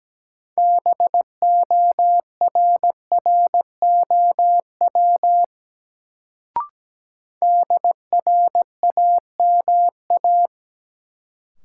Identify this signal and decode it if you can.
Morse code, heard as a keyed tone, about 17 wpm, 700 Hz. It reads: BORROW DRAMA
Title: Sentences from Top 1500 Words - New Word Drill - 17wpm